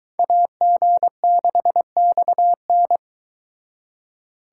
Morse code, heard as a keyed tone, about 23 wpm, 700 Hz. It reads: AG6XN